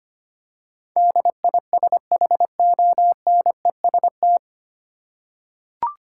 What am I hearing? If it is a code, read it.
Morse code, heard as a keyed tone, about 25 wpm, 700 Hz. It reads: DISHONEST
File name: Nine Letter Words - 25wpm